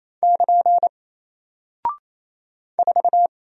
Morse code, heard as a keyed tone, about 27 wpm, 700 Hz. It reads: <KN> 4
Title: Single Character (Letter:Number:Punctuation:Prosign) Alt 28wpm